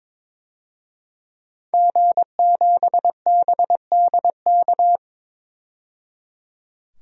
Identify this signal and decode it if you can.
Morse code, heard as a keyed tone, about 22 wpm, 700 Hz. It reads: G7BDK
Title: DXCC-100 Call Signs - 5 Characters - 22wpm